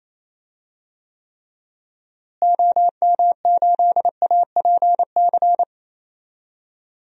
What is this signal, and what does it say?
Morse code, heard as a keyed tone, about 27 wpm, 700 Hz. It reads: OM8APC
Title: DXCC-100 Call Signs - 6 Characters - Repeated 3x - 28wpm